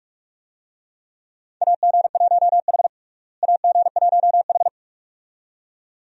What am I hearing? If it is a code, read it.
Morse code, heard as a keyed tone, about 41 wpm, 700 Hz. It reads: AG1H AG1H
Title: USA Call Signs - 4 Characters - Repeated 2x - 45wpm